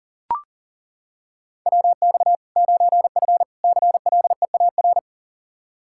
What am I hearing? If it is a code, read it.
Morse code, heard as a keyed tone, about 40 wpm, 700 Hz. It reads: WX 9F CLEAR
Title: K1USN SST QSO Practice 40wpm